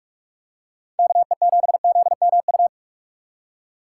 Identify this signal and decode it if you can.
Morse code, heard as a keyed tone, about 41 wpm, 700 Hz. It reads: K E 7 Z M U
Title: USA Call Signs - 6 Characters - 6x Character Spacing - 45wpm